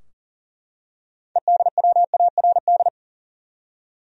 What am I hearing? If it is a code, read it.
Morse code, heard as a keyed tone, about 40 wpm, 700 Hz. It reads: EDWARD
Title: USA Common Names - 40wpm